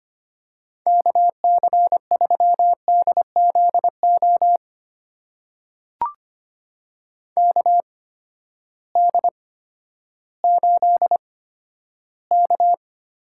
Morse code, heard as a keyed tone, about 25 wpm, spaced as for 12 wpm, 700 Hz. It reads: KC3DZO KD8K
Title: USA Call Signs - 6 Characters - 8x Character Spacing - 25wpm